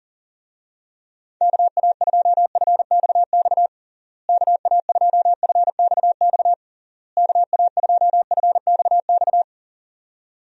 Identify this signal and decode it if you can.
Morse code, heard as a keyed tone, about 37 wpm, 700 Hz. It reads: KA2FXX KA2FXX KA2FXX
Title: USA Call Signs - 6 Characters - Repeated 3x - 40wpm